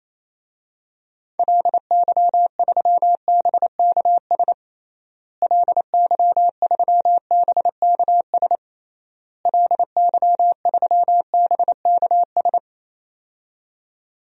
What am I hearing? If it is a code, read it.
Morse code, heard as a keyed tone, about 27 wpm, 700 Hz. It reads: LY3BKS LY3BKS LY3BKS
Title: DXCC-100 Call Signs - 6 Characters - Repeated 3x - 28wpm